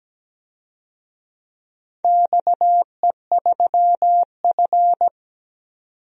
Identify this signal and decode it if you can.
Morse code, heard as a keyed tone, about 17 wpm, 700 Hz. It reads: XE3F
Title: DXCC-100 Call Signs - 4 Characters - 17wpm